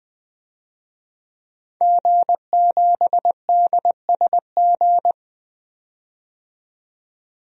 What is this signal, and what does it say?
Morse code, heard as a keyed tone, about 20 wpm, 700 Hz. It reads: G7DSG
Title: DXCC-100 Call Signs - 5 Characters - 20wpm